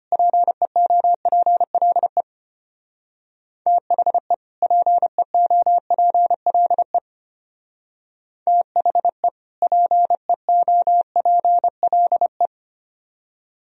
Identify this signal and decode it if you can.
Morse code, heard as a keyed tone, about 31 wpm, 700 Hz. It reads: PEOPLE THE PEOPLE THE PEOPLE
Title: Speed-Racing - Sets of 2 Words 25wpm